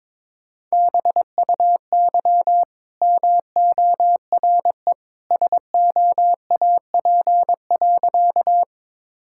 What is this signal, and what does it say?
Morse code, heard as a keyed tone, about 22 wpm, 690 Hz. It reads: BUY MORE SOAP.
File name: practice / 22wpm / Easy Sentences with Hints - 22wpm